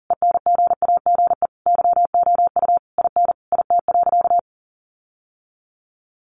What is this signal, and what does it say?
Morse code, heard as a keyed tone, about 40 wpm, 700 Hz. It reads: ENGAGE YOU IN IT.
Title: Sentences from Top 1500 Words 40wpm